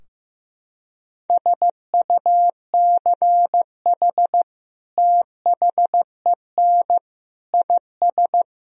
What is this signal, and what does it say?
Morse code, heard as a keyed tone, about 16 wpm, 700 Hz. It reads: SUCH THEN IS
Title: Sentences from Top 1000 Words - Review Part 4 - 15wpm